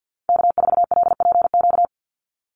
Morse code, heard as a keyed tone, about 50 wpm, 700 Hz. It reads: K4LPQ
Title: NA SOTA Activator Callsigns 50wpm